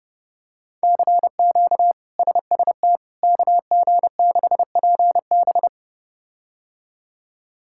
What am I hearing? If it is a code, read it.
Morse code, heard as a keyed tone, about 30 wpm, 700 Hz. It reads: CQ SST KG6PB
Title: K1USN SST QSO Practice - 30wpm